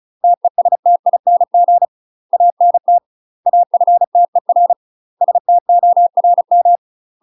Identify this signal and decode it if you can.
Morse code, heard as a keyed tone, about 34 wpm, 700 Hz. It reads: TESTING ANT AFTER STORM
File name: UK MASTER QSOs TRAINING / 35WPM / UK MASTER QSOs 0006